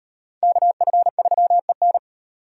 Morse code, heard as a keyed tone, about 38 wpm, 700 Hz. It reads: KF3EN